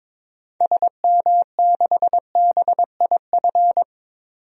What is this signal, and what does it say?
Morse code, heard as a keyed tone, about 22 wpm, 700 Hz. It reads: SM6BIF
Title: DXCC-100 Call Signs - 6 Characters - 22wpm